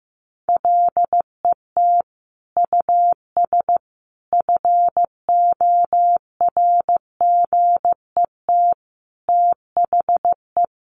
Morse code, heard as a keyed tone, about 15 wpm, 700 Hz. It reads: LET US FORGET THE
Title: Sentences from Top 1200 Words 15wpm